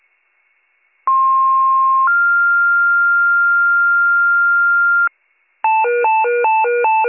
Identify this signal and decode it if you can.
Two-tone page, heard as a one-second tone, then a three-second tone, about 1060 then 1400 Hz, with noise. It is heard from 1.1 to 5.1 s.